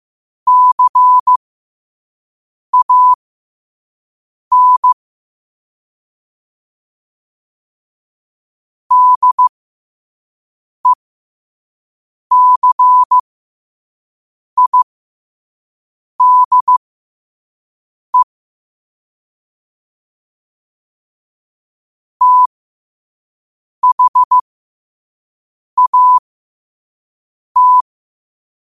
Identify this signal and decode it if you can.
Morse code, heard as a keyed tone, about 15 wpm, spaced as for 5 wpm, 1000 Hz. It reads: CAN DECIDE THAT
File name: qotdfiles / 5wpm